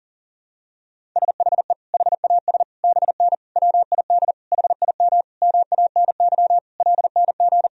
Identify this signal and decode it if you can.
Morse code, heard as a keyed tone, about 40 wpm, 700 Hz. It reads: SHE HAS BN WID HIM MANY LNG
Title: Sentences from Top 200 Words and Abbreviations 40wpm